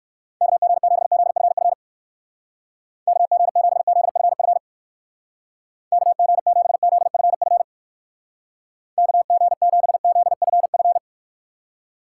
Morse code, heard as a keyed tone, about 57 wpm, 700 Hz. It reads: KG7ZFF KG7ZFF KG7ZFF KG7ZFF